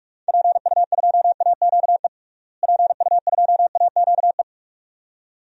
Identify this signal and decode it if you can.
Morse code, heard as a keyed tone, about 46 wpm, 700 Hz. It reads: PU2AQE PU2AQE
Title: DXCC-100 Call Signs - 6 Characters - Repeated 2x - 45wpm